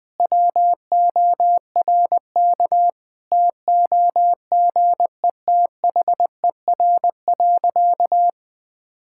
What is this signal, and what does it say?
Morse code, heard as a keyed tone, about 19 wpm, 700 Hz. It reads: WORK TOGETHER.
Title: Sentences with 7-Word Phrases - 20wpm